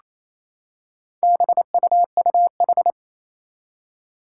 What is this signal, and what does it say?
Morse code, heard as a keyed tone, about 28 wpm, 700 Hz. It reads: BUUH
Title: Four Challenging Letter-Number Combinations 28wpm